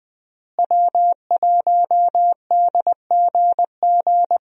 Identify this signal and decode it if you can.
Morse code, heard as a keyed tone, about 20 wpm, 700 Hz. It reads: W1DGG
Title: USA Call Signs - 5 Characters - Repeated 3x - 20wpm